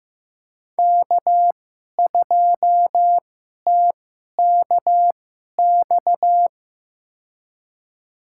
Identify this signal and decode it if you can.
Morse code, heard as a keyed tone, about 15 wpm, spaced as for 11 wpm, 700 Hz. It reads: K2TKX
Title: USA Call Signs - 2x Character Spacing - 15wpm